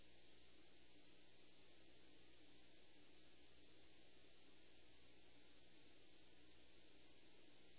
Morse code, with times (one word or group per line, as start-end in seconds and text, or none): none